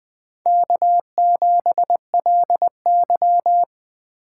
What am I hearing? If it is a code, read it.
Morse code, heard as a keyed tone, about 20 wpm, 700 Hz. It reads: K7LY
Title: USA Call Signs - 4 Characters - 3x Character Spacing - 20wpm